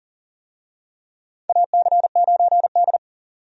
Morse code, heard as a keyed tone, about 40 wpm, 700 Hz. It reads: AC9D
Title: USA Call Signs - 4 Characters - 40wpm